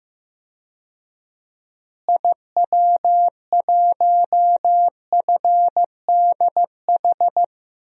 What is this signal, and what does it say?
Morse code, heard as a keyed tone, about 15 wpm, 700 Hz. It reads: IW1FDH
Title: DXCC-100 Call Signs - 6 Characters - Repeated 3x - 15wpm